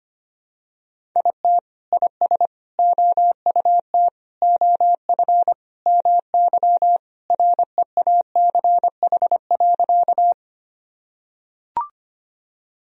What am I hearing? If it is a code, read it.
Morse code, heard as a keyed tone, about 25 wpm, 700 Hz. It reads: IT IS OUT OF MY REACH.
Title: Sentences from Top 1000 Words - Review Part 2 - 25wpm